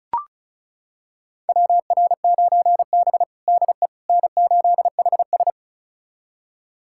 Morse code, heard as a keyed tone, about 36 wpm, 700 Hz. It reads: WR9B DE N8HS
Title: USA Callsign DE Callsign - 4 Characters - 35wpm